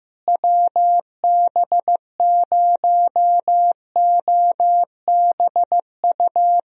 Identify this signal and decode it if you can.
Morse code, heard as a keyed tone, about 15 wpm, 700 Hz. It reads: WB0OBU
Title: USA Call Signs - 6 Characters - Repeated 3x - 15wpm